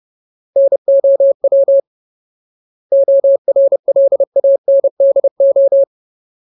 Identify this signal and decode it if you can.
Morse code, heard as a keyed tone, about 30 wpm, 550 Hz. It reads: NOW ORLANDO
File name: Ultimate Word List / 30WPM Ultimate